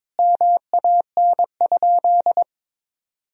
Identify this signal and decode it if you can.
Morse code, heard as a keyed tone, about 22 wpm, 700 Hz. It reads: MAN?
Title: Sentences from Top 1000 Words - Review Part 3 - 22wpm